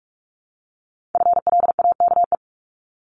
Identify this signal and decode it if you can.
Morse code, heard as a keyed tone, about 46 wpm, 700 Hz. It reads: FLAKE
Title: Multi-letter Phonemes Repeated 3x 45wpm